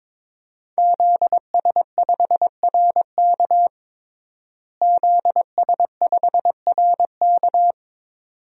Morse code, heard as a keyed tone, about 22 wpm, 700 Hz. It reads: ZS5RK ZS5RK
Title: DXCC-100 Call Signs - 5 Characters - Repeated 2x - 22wpm